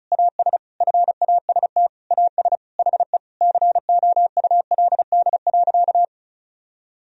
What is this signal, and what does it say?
Morse code, heard as a keyed tone, about 34 wpm, 700 Hz. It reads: AS FAST AS HE COULD.
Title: Sentences from Top 1000 Words - Review Part 2 - 35wpm